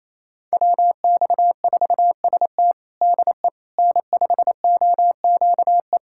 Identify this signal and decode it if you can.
Morse code, heard as a keyed tone, about 28 wpm, 700 Hz. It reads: WX4ST DE N5OQE
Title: USA Callsign DE Callsign 28wpm